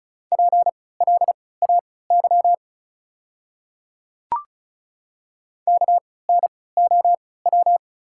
Morse code, heard as a keyed tone, about 36 wpm, spaced as for 20 wpm, 700 Hz. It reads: PLAY KNOW
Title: 3x Character Spacing - Top 200 Words - 35wpm